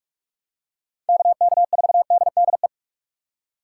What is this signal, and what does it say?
Morse code, heard as a keyed tone, about 46 wpm, 700 Hz. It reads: KK4DDE
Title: USA Call Signs - 6 Characters - 2x Character Spacing - 45wpm